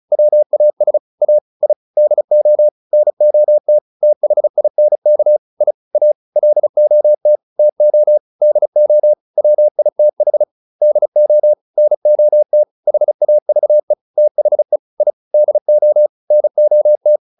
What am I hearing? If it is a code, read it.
Morse code, heard as a keyed tone, about 34 wpm, 600 Hz. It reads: WAS A I DO NOT THINK I A LOT TO DO WITH DO NOT HAVE THE I DO NOT
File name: WD403 35